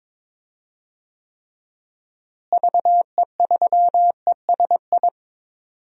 Morse code, heard as a keyed tone, about 22 wpm, 700 Hz. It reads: VE3ESI
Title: DXCC-100 Call Signs - 6 Characters - Repeated 2x - 22wpm